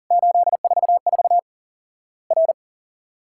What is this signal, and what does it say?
Morse code, heard as a keyed tone, about 39 wpm, 700 Hz. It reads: 844 R
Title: Worked All Europe DX Contest - QTC Exchange - 40wpm